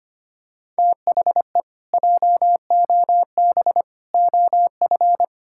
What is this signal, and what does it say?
Morse code, heard as a keyed tone, about 25 wpm, 700 Hz. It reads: THE JOB OF
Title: Pangram Challenge v2 - 25wpm